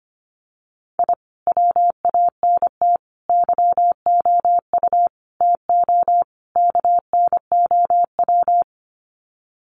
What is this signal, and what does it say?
Morse code, heard as a keyed tone, about 25 wpm, 700 Hz. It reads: I WANT YOU TO KNOW
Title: Sets of 5 Words Encore 25wpm